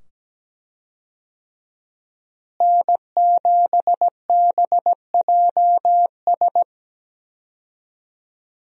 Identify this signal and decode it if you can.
Morse code, heard as a keyed tone, about 17 wpm, 700 Hz. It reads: N7BJS
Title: USA Call Signs - 5 Characters - Repeated 2x - 17wpm